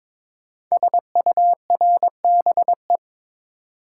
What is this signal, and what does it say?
Morse code, heard as a keyed tone, about 22 wpm, 700 Hz. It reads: SURBE
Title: Code Groups - 5 Letters - 22wpm